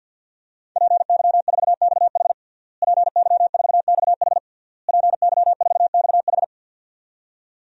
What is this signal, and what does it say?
Morse code, heard as a keyed tone, about 48 wpm, 700 Hz. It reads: PY4XH PY4XH PY4XH
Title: DXCC-100 Call Signs - 5 Characters - Repeated 3x - 50wpm